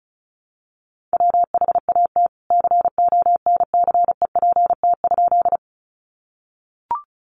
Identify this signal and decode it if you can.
Morse code, heard as a keyed tone, about 34 wpm, 700 Hz. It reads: WHAT CONCEPT?
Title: Sentences from Top 1100 Words 35wpm